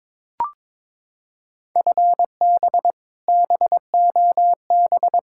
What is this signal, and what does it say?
Morse code, heard as a keyed tone, about 22 wpm, 700 Hz. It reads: FB BOB
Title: K1USN SST QSO Practice 22wpm